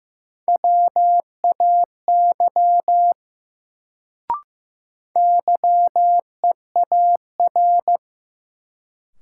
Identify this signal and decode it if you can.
Morse code, heard as a keyed tone, about 15 wpm, 700 Hz. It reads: WAY YEAR
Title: Sets of 2 Words - New Word Drill - 15wpm